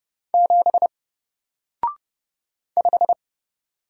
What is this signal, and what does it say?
Morse code, heard as a keyed tone, about 31 wpm, 700 Hz. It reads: 7 5
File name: Single-Letter-Number-30wpm